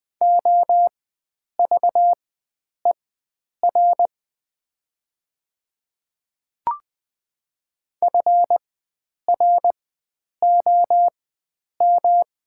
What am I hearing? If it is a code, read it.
Morse code, heard as a keyed tone, about 20 wpm, spaced as for 9 wpm, 700 Hz. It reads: OVER FROM
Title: 4x Character Spacing - Top 500 Words - 20wpm